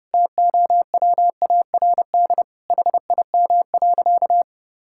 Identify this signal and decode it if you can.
Morse code, heard as a keyed tone, about 32 wpm, 700 Hz. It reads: TOWARD HIM.
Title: Sentences from Top 1000 Words - Review Part 6 - 30wpm